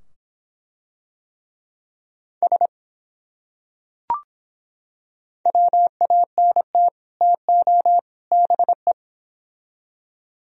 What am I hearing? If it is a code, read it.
Morse code, heard as a keyed tone, about 24 wpm, 700 Hz. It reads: S WANT TO BE